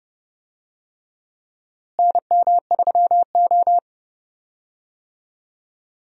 Morse code, heard as a keyed tone, about 30 wpm, 700 Hz. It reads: NM3O
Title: USA Call Signs - 4 Characters - Repeated 3x - 30wpm